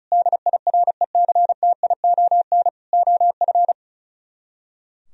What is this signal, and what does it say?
Morse code, heard as a keyed tone, about 44 wpm, 700 Hz. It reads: DIRECTION OF